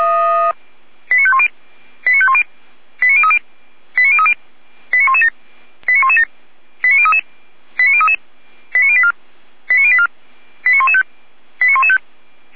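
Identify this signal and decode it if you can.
Five-tone selective call (ZVEI), heard as quick runs of five tones, tones of about 70 ms. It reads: 89510 89510 89030 89030 89117 89117 89033 89033 89084 89084 89115 89115
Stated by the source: ZVEI five-tone selective call received on an SDR.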